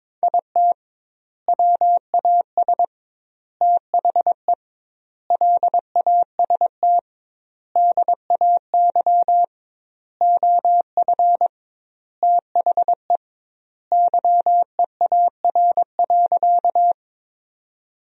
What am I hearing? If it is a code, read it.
Morse code, heard as a keyed tone, about 22 wpm, 700 Hz. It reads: IT WAS THE LAST DAY OF THE YEAR.
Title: 2x Word-Spacing - Sentences from Top 300 Words - 22wpm